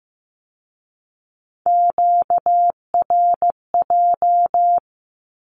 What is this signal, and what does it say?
Morse code, heard as a keyed tone, about 15 wpm, 700 Hz. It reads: QRJ
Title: Q-Codes with Spoken Meaning - 15wpm